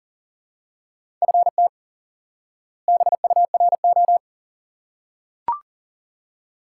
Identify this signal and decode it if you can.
Morse code, heard as a keyed tone, about 39 wpm, 700 Hz. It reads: FT BURO
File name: course / instant-qso-element-course-3 / IQE Course 3 - Lesson 89 - YAGI - Two Elements - 40wpm